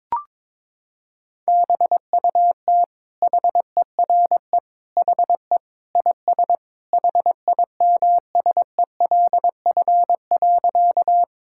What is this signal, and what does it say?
Morse code, heard as a keyed tone, about 22 wpm, 700 Hz. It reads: BUT HERE HE IS HIMSELF.
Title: Sentences from Top 1600 Words 22wpm